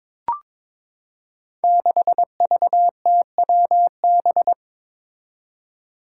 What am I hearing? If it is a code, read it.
Morse code, heard as a keyed tone, about 22 wpm, 700 Hz. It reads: 6VTWB
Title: Five Challenging Letter-Number Combinations 22wpm